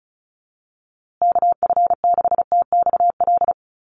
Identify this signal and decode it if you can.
Morse code, heard as a keyed tone, about 34 wpm, 700 Hz. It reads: KF6TXL
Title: USA Call Signs Repeated 3x 35wpm